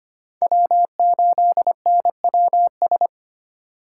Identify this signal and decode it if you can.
Morse code, heard as a keyed tone, about 29 wpm, 700 Hz. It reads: W8NWS
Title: Speed Racing - USA 1x3 Call Signs 25wpm